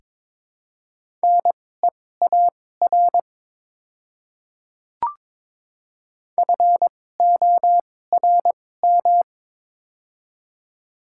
Morse code, heard as a keyed tone, about 22 wpm, spaced as for 16 wpm, 700 Hz. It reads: NEAR FORM
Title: 2x Character Spacing - Top 200 Words - 22wpm